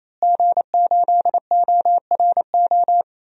Morse code, heard as a keyed tone, about 28 wpm, 700 Hz. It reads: G8ORO
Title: DXCC-100 Call Signs - 5 Characters - Repeated 3x - 28wpm